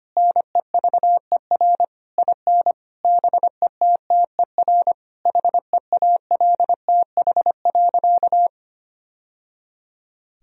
Morse code, heard as a keyed tone, about 25 wpm, 700 Hz. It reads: NEVER IN BETTER HEALTH.
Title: Sentences from Top 1000 Words - Review Part 3 - 25wpm